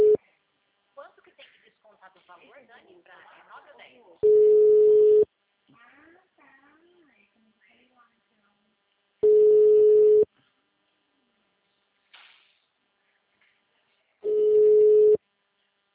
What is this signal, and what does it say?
Ringback tone (Europe, 425 Hz), with speech.
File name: out- -2019- - - .1331